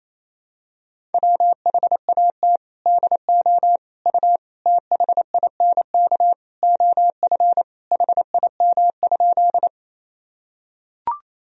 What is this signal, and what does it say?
Morse code, heard as a keyed tone, about 28 wpm, 700 Hz. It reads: WHAT DO U THINK OF HIM?
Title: Sentences from Top 200 Words and Abbreviations 28wpm